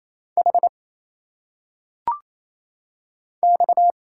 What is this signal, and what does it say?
Morse code, heard as a keyed tone, about 28 wpm, 700 Hz. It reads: H X
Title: Single Letters - Phonetic Pronunciation 28wpm